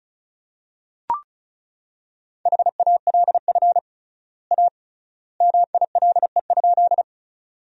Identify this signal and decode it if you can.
Morse code, heard as a keyed tone, about 36 wpm, 700 Hz. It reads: HALF A MILE?
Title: 3x Word-Spacing - Sentences from Top 400 Words - 35wpm